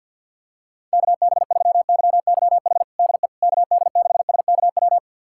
Morse code, heard as a keyed tone, about 52 wpm, 700 Hz. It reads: KB3YYH DE KD6SKW
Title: USA Callsign DE Callsign - 6 Characters - 50wpm